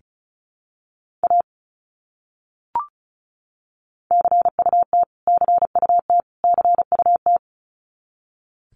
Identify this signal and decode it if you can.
Morse code, heard as a keyed tone, about 34 wpm, 700 Hz. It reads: A CUT CUT CUT
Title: Top 300 Words Repeated 35wpm